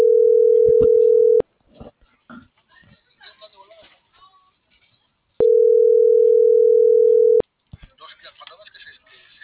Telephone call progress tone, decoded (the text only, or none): ringback tone